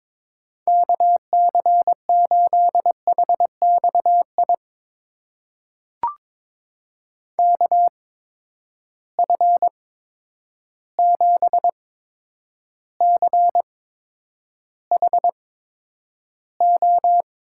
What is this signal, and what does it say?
Morse code, heard as a keyed tone, about 22 wpm, spaced as for 9 wpm, 700 Hz. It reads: KC8HXI KF7CHO
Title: USA Call Signs - 6 Characters - 8x Character Spacing - 22wpm